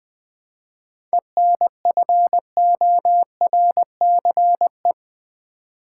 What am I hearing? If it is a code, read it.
Morse code, heard as a keyed tone, about 21 wpm, 700 Hz. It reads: ENFORCE
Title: 7 Letter Words - First 3 Letters Given - 20wpm